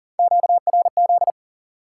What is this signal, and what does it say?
Morse code, heard as a keyed tone, about 40 wpm, 700 Hz. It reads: QRZ
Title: Top 90 QSO Elements 40wpm